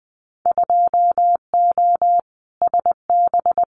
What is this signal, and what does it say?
Morse code, heard as a keyed tone, about 20 wpm, 700 Hz. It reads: 2O SB
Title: Winter Field Day Exchanges 20wpm